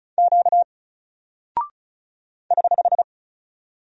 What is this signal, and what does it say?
Morse code, heard as a keyed tone, about 36 wpm, 700 Hz. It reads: Q <HH>
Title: Single Character (Letter:Number:Punctuation:Prosign) 35wpm